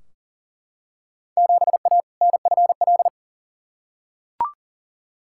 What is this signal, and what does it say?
Morse code, heard as a keyed tone, about 37 wpm, 700 Hz. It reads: 7A NFL
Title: Field Day Exchanges Encore - 40wpm